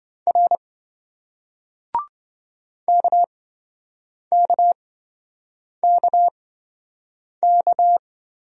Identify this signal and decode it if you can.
Morse code, heard as a keyed tone, about 27 wpm, spaced as for 7 wpm, 700 Hz. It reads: R KKKK